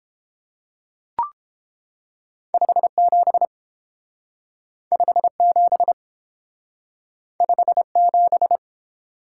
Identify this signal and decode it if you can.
Morse code, heard as a keyed tone, about 29 wpm, 700 Hz. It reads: 57 57 57